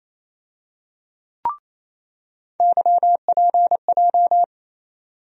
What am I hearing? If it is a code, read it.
Morse code, heard as a keyed tone, about 28 wpm, 700 Hz. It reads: YPJ
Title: USA Call Sign Postfixes - 3 Letters 28wpm